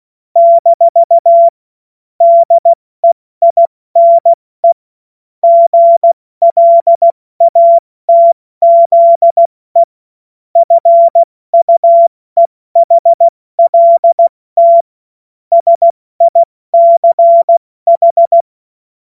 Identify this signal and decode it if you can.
Morse code, heard as a keyed tone, about 16 wpm, 680 Hz. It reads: - DEINE GLATZE FUEHLT SICH
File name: Morsekurs / Witze / Witz 15. Februar 2022 70 F80